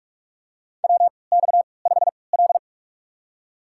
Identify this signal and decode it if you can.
Morse code, heard as a keyed tone, about 47 wpm, spaced as for 26 wpm, 700 Hz. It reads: WX5L
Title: USA Call Signs - 3x Character Spacing - 45wpm